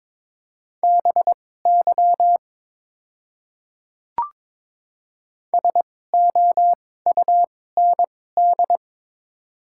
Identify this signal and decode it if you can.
Morse code, heard as a keyed tone, about 22 wpm, spaced as for 16 wpm, 700 Hz. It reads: BY SOUND
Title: 2x Character Spacing - Top 200 Words - 22wpm